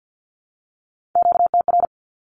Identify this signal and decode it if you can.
Morse code, heard as a keyed tone, about 53 wpm, 700 Hz. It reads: QTR
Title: Q-Codes 50wpm